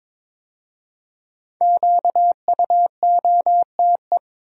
Morse code, heard as a keyed tone, about 22 wpm, 700 Hz. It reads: QUOTE
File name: 5 Letter Words - First 3 Letters Given - 22wpm